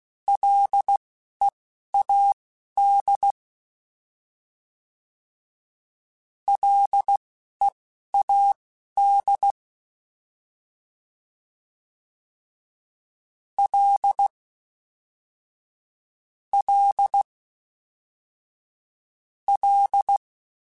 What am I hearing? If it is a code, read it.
Morse code, heard as a keyed tone, about 16 wpm, spaced as for 12 wpm, 800 Hz. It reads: LEAD LEAD L L L
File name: Ls / Single L m3s / Single L Prefixe 3x2 Twice @10